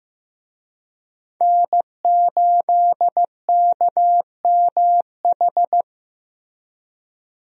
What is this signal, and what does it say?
Morse code, heard as a keyed tone, about 15 wpm, 700 Hz. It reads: N8KMH
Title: USA Call Signs - 5 Characters - Repeated 3x - 15wpm